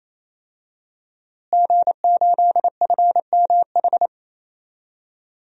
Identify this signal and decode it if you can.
Morse code, heard as a keyed tone, about 28 wpm, 700 Hz. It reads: G8FMH
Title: DXCC-100 Call Signs - 5 Characters - Repeated 3x - 28wpm